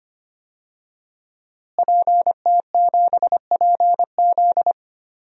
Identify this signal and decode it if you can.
Morse code, heard as a keyed tone, about 25 wpm, 700 Hz. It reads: PT7PZ
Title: DXCC-100 Call Signs - 5 Characters - Repeated 2x - 25wpm